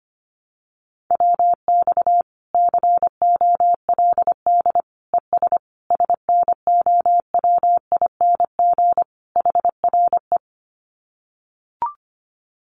Morse code, heard as a keyed tone, about 25 wpm, 700 Hz. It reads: WX COLD ES SNOWING HRE
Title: Common QSO Phrases 25wpm